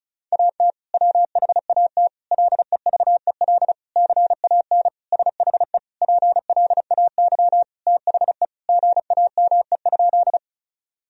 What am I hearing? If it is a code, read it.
Morse code, heard as a keyed tone, about 36 wpm, 700 Hz. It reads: AT WHAT LEVEL CAN SHE PLAY THE GAME?
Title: Sentences from Top 1000 Words - Review Part 4 - 35wpm